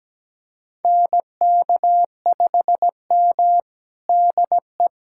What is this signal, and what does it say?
Morse code, heard as a keyed tone, about 17 wpm, 700 Hz. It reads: NK5M DE
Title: USA Callsign DE Callsign - 4 Characters - 17wpm